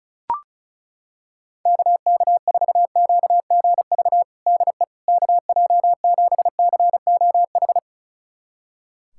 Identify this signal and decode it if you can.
Morse code, heard as a keyed tone, about 34 wpm, 700 Hz. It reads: KK4QGV DE KJ7COH
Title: USA Callsign DE Callsign 35wpm